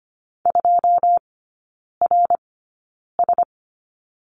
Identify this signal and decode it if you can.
Morse code, heard as a keyed tone, about 25 wpm, spaced as for 9 wpm, 700 Hz. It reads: 2RS
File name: course / 25:10wpm / 093 Lesson 16e - Letter c - Three Characters 25wpm